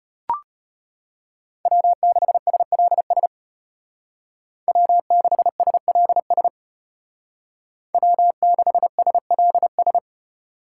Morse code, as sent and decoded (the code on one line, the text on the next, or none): .-- -.... ... .-.. ... / .-- -.... ... .-.. ... / .-- -.... ... .-.. ...
W6SLS W6SLS W6SLS